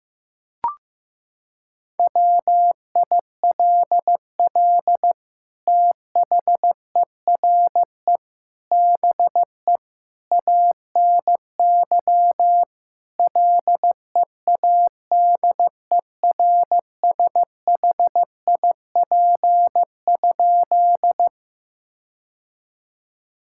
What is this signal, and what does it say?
Morse code, heard as a keyed tone, about 15 wpm, 700 Hz. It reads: WILL THERE BE ANY LEADERSHIP?
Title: Sentences from Top 1600 Words 15wpm